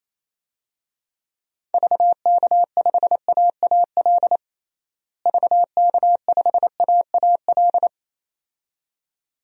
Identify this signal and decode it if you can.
Morse code, heard as a keyed tone, about 27 wpm, 700 Hz. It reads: VK5AAL VK5AAL
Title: DXCC-100 Call Signs - 6 Characters - Repeated 2x - 28wpm